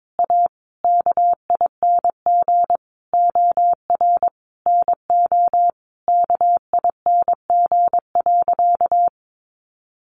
Morse code, heard as a keyed tone, about 22 wpm, 700 Hz. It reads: A KING OR NO KING.
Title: Sentences from Top 600 Words 22wpm